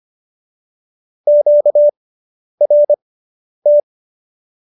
Morse code, heard as a keyed tone, about 25 wpm, spaced as for 10 wpm, 590 Hz. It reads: QRT